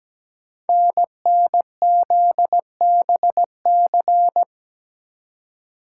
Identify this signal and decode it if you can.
Morse code, heard as a keyed tone, about 17 wpm, 700 Hz. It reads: NNZBC
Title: Code Groups - 5 Letters - 17wpm